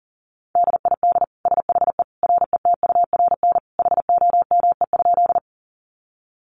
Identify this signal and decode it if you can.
Morse code, heard as a keyed tone, about 40 wpm, 700 Hz. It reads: DID SHE RETURN HOME?
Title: Sentences from Top 800 Words 40wpm